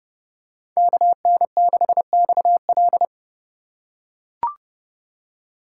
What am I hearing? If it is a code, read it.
Morse code, heard as a keyed tone, about 30 wpm, 700 Hz. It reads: KN6XL
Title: USA Call Signs - 5 Characters - 4x Character Spacing - 30wpm